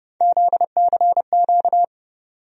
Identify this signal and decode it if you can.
Morse code, heard as a keyed tone, about 31 wpm, 700 Hz. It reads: ZCQ
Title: USA Call Sign Postfixes - 3 Letters 30wpm